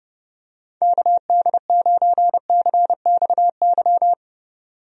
Morse code, heard as a keyed tone, about 30 wpm, 700 Hz. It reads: KD9CXY
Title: USA Call Signs - 6 Characters - 30wpm